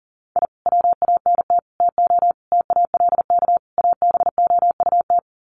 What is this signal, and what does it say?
Morse code, heard as a keyed tone, about 40 wpm, 700 Hz. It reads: I WANT TO TALK ABOUT
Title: Sets of 5 Words Encore 40wpm